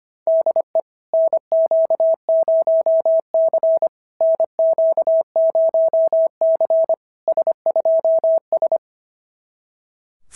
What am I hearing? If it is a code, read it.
Morse code, heard as a keyed tone, about 25 wpm, 690 Hz. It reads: DE NQ0C NQ0C S2S
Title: SOTA QSO Practice 25wpm